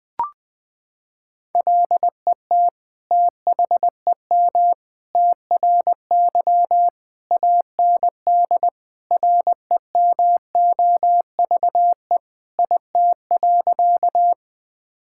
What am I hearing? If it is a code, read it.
Morse code, heard as a keyed tone, about 21 wpm, 700 Hz. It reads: LET THEM TRY AND REMOVE IT.
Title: Sentences from Top 1000 Words - Review Part 5 - 20wpm